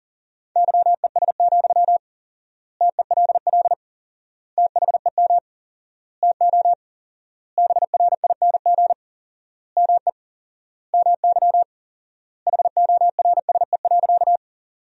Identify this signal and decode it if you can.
Morse code, heard as a keyed tone, about 40 wpm, 700 Hz. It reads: YES, TELL THEM TO BRING ME MY HORSE.
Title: 4x Word-Spacing - Sentences from Top 500 Words - 40wpm